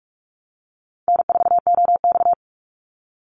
Morse code, heard as a keyed tone, about 44 wpm, 700 Hz. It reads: N4OX
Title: Easy Call Signs Repeated 3x - 45wpm